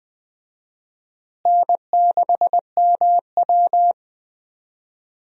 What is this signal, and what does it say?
Morse code, heard as a keyed tone, about 20 wpm, 700 Hz. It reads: N6MW
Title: USA Call Signs - 4 Characters - Repeated 3x - 20wpm